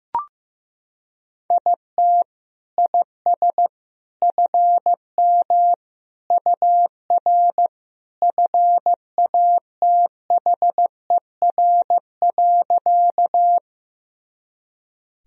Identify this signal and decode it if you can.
Morse code, heard as a keyed tone, about 16 wpm, 700 Hz. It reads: IT IS FM UR FATHER.
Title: Sentences from Top 300 Words and Abbreviations 15wpm